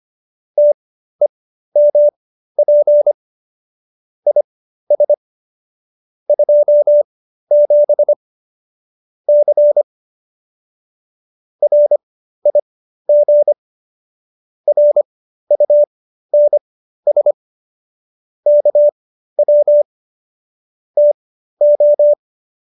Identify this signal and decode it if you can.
Morse code, heard as a keyed tone, about 25 wpm, spaced as for 13 wpm, 590 Hz. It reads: TEMP IS 27 C RIG RUNS KW TO